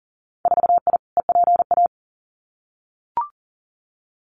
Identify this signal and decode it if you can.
Morse code, heard as a keyed tone, about 41 wpm, 700 Hz. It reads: 4I EPA
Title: Winter Field Day Exchanges Encore 40wpm